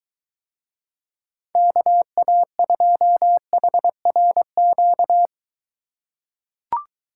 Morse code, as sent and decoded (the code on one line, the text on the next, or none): -.- .- ..--- .... .-. --.-
KA2HRQ